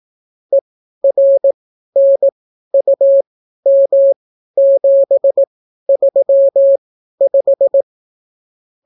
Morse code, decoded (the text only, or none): ERNUM735